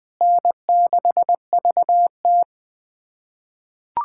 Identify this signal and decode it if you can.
Morse code, heard as a keyed tone, about 20 wpm, 700 Hz. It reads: N6VT
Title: USA Callsign DE Callsign - 4 Characters - 20wpm